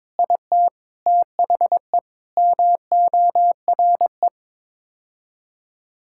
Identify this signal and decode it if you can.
Morse code, heard as a keyed tone, about 22 wpm, 700 Hz. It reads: IT THE MORE
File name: practice / 22wpm / Sets of 8 Words - 22wpm